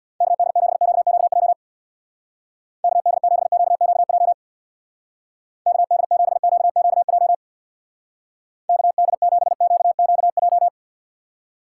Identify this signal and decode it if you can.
Morse code, heard as a keyed tone, about 61 wpm, 700 Hz. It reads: KD7QQJ KD7QQJ KD7QQJ KD7QQJ
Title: Speed Racing - USA Call Signs 50wpm